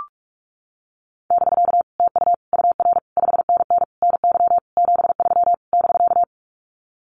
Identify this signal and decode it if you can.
Morse code, heard as a keyed tone, about 44 wpm, 700 Hz. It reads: <BK> TU UR 5NN NY 73 <BK>
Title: Common QSO Phrases Encore 45wpm